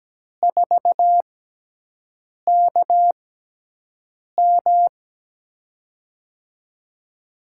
Morse code, heard as a keyed tone, about 17 wpm, spaced as for 6 wpm, 700 Hz. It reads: 4KM
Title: USA Call Signs - 5 Characters - 6x Character Spacing - 17wpm